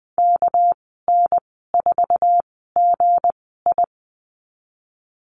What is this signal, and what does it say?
Morse code, heard as a keyed tone, about 20 wpm, spaced as for 15 wpm, 700 Hz. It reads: KN4GI
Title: 200 Lesson 32g - Letter k - Call Signs 20wpm